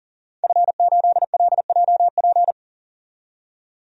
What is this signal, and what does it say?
Morse code, heard as a keyed tone, about 40 wpm, 700 Hz. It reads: F8LJP
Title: DXCC-100 Call Signs - 5 Characters - 40wpm